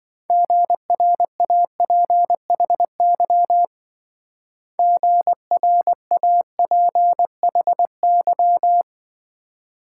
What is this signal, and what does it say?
Morse code, heard as a keyed tone, about 26 wpm, 700 Hz. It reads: GRAPHY GRAPHY